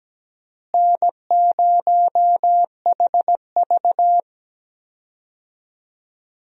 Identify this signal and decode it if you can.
Morse code, heard as a keyed tone, about 17 wpm, 700 Hz. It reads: N0HV
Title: USA Call Signs - 4 Characters - Repeated 3x - 17wpm